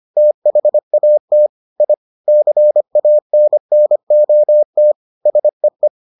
Morse code, heard as a keyed tone, about 25 wpm, 600 Hz. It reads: THAT I CANNOT SEE